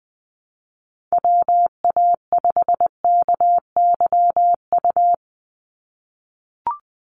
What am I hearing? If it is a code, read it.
Morse code, heard as a keyed tone, about 20 wpm, 700 Hz. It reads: WA5KYU
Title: USA Call Signs 20wpm